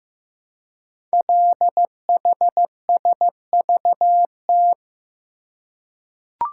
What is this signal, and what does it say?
Morse code, heard as a keyed tone, about 15 wpm, 700 Hz. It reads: LHSVT
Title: Code Groups - 5 Letters - 15wpm